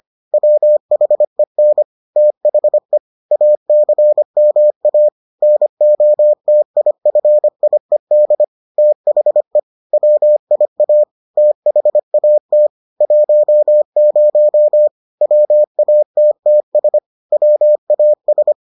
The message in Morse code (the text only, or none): WHEN THE ACMA NOTIFIED THE WIA THAT 10 WATTS WAS